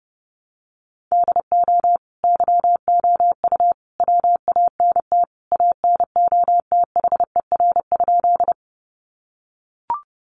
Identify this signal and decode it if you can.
Morse code, heard as a keyed tone, about 30 wpm, 700 Hz. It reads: DO YOU WANT ANOTHER?
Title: Sentences from Top 600 Words 30wpm